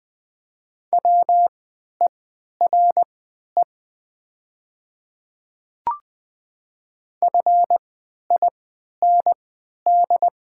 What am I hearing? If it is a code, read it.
Morse code, heard as a keyed tone, about 20 wpm, spaced as for 11 wpm, 700 Hz. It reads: WERE FIND
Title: 3x Character Spacing - Top 100 Words - 20wpm